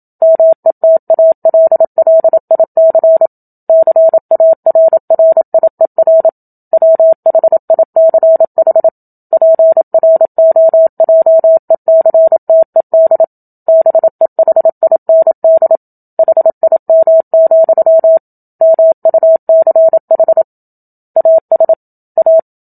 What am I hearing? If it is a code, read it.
Morse code, heard as a keyed tone, about 27 wpm, 650 Hz. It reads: METALLIC CARRIER WHICH PROJECTED BEHIND HIM, MUCH AS A
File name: Book Two / 01-Chapter 1 - 27 WPM